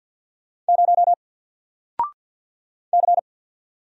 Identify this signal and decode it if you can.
Morse code, heard as a keyed tone, about 48 wpm, 700 Hz. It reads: 0 C
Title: Single Letter-Number Alt 50wpm